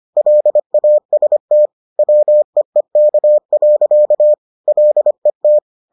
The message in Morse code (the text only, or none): LAST WEEK. LET